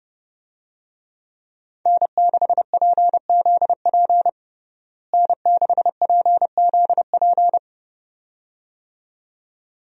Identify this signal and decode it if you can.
Morse code, heard as a keyed tone, about 30 wpm, 700 Hz. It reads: N6PZP N6PZP
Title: USA Call Signs - 5 Characters - Repeated 2x - 30wpm